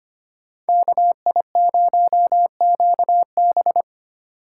Morse code, heard as a keyed tone, about 25 wpm, 700 Hz. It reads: KI0QB
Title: USA Call Signs - 5 Characters - 3x Character Spacing - 25wpm